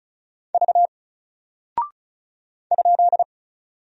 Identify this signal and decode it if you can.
Morse code, heard as a keyed tone, about 36 wpm, 700 Hz. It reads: V ?
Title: Single Character (Letter:Number:Punctuation:Prosign) Alt 35wpm